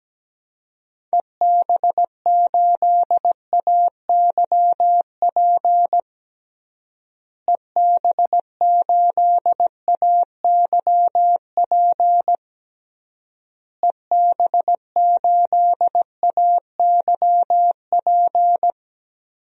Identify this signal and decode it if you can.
Morse code, heard as a keyed tone, about 17 wpm, 700 Hz. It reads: EB8AYP EB8AYP EB8AYP
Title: DXCC-100 Call Signs - 6 Characters - Repeated 3x - 17wpm